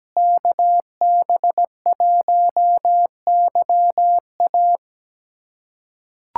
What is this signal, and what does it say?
Morse code, heard as a keyed tone, about 17 wpm, 700 Hz. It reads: KB1YA
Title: USA Callsign DE Callsign 17wpm